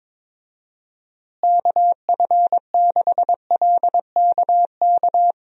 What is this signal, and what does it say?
Morse code, heard as a keyed tone, about 22 wpm, 700 Hz. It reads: KF6LKK
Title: USA Call Signs - 6 Characters - Repeated 2x - 22wpm